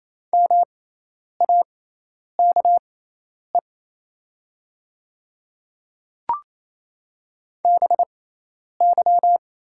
Morse code, heard as a keyed tone, about 28 wpm, spaced as for 10 wpm, 700 Hz. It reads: MAKE BY
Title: 6x Character Spacing - Top 200 Words - 28wpm